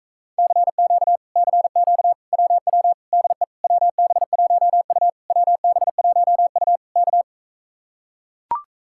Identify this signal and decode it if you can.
Morse code, heard as a keyed tone, about 42 wpm, 700 Hz. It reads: CQ CQ WW DE WB1U WB1U K